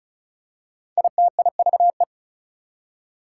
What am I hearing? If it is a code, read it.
Morse code, heard as a keyed tone, about 36 wpm, 700 Hz. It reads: ITIVE
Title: Prefixes and Suffixes 35wpm